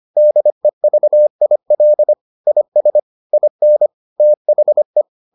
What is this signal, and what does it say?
Morse code, heard as a keyed tone, about 25 wpm, 600 Hz. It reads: DEVIL IS IN THE